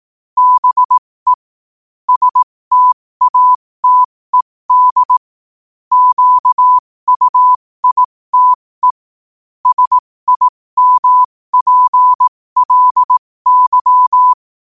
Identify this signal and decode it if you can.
Morse code, heard as a keyed tone, about 18 wpm, spaced as for 15 wpm, 1000 Hz. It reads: BE STATED QUITE SIMPLY
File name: qotdfiles / 15wpm